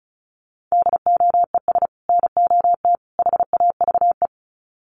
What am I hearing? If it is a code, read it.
Morse code, heard as a keyed tone, about 34 wpm, 700 Hz. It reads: DOES NOT HAVE
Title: Sets of 3 Words 35wpm